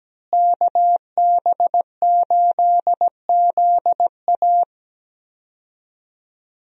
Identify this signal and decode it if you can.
Morse code, heard as a keyed tone, about 17 wpm, 700 Hz. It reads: KB8ZA
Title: USA Call Signs - 5 Characters - Repeated 3x - 17wpm